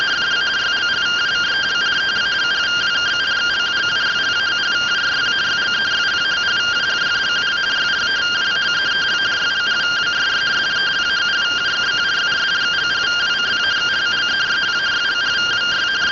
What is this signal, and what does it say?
Two-tone teleprinter signal, about 170 Hz shift.